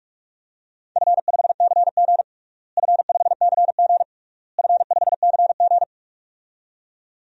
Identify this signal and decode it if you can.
Morse code, heard as a keyed tone, about 46 wpm, 700 Hz. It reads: F5CG F5CG F5CG
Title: DXCC-100 Call Signs - 4 Characters - Repeated 3x - 45wpm